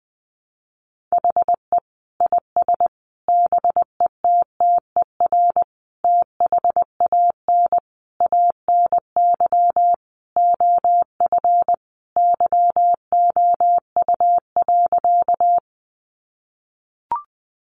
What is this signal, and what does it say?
Morse code, heard as a keyed tone, about 20 wpm, 700 Hz. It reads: HE IS BETTER THAN ANY OF YOU.
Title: Sentences from Top 400 Words 20wpm